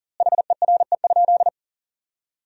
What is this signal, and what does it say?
Morse code, heard as a keyed tone, about 37 wpm, 700 Hz. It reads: HERE?
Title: 2x Word-Spacing - Sentences from Top 400 Words - 40wpm